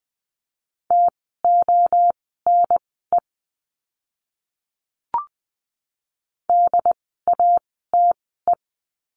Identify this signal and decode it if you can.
Morse code, heard as a keyed tone, about 20 wpm, spaced as for 15 wpm, 700 Hz. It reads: TONE DATE
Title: 068 Lesson 12f - Letter d - Words 20wpm